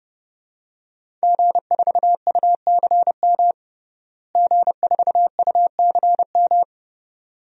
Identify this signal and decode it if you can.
Morse code, heard as a keyed tone, about 31 wpm, 700 Hz. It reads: G4UCM G4UCM
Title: DXCC-100 Call Signs - 5 Characters - Repeated 2x - 30wpm